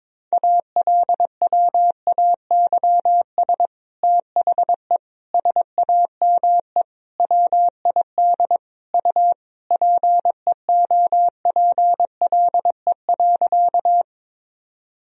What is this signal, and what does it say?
Morse code, heard as a keyed tone, about 22 wpm, 700 Hz. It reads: ALWAYS THE SAME WID U PEOPLE.
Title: Sentences from Top 300 Words and Abbreviations 22wpm